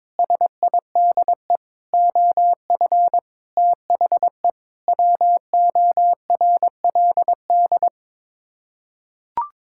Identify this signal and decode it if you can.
Morse code, heard as a keyed tone, about 22 wpm, 700 Hz. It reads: SIDE OF THE WORLD
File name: practice / 22wpm / Sets of 7 Words - 22wpm